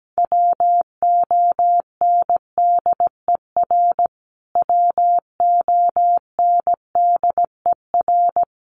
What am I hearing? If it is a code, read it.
Morse code, heard as a keyed tone, about 17 wpm, 700 Hz. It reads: WONDER WONDER
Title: Top 500 Words Repeated 17wpm